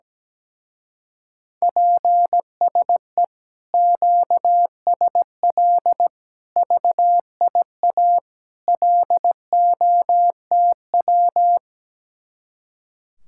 Morse code, heard as a keyed tone, about 17 wpm, 700 Hz. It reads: PSE QSL VIA LOTW
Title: Speed-Racing Common QSO Phrases - 17wpm